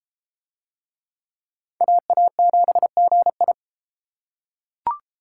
Morse code, heard as a keyed tone, about 30 wpm, 700 Hz. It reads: AA7GI